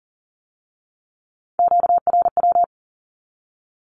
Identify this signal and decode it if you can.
Morse code, heard as a keyed tone, about 40 wpm, 700 Hz. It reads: QRW
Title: Q-Codes with Spoken Meaning - 40wpm